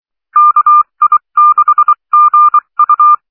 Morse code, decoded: KI6GU